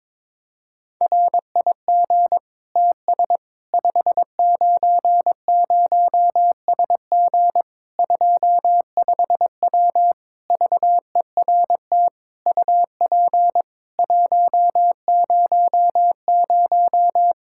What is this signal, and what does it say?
Morse code, heard as a keyed tone, about 21 wpm, 700 Hz. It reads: RIG TS 590SG 25W VERT UP 100
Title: Practice QSO - Exchange 2 - Info - 22wpm